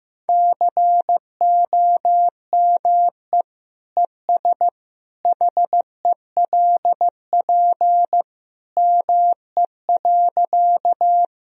Morse code, heard as a keyed tone, about 15 wpm, 700 Hz. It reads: COME ES HELP ME.
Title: Sentences from Top 200 Words and Abbreviations 15wpm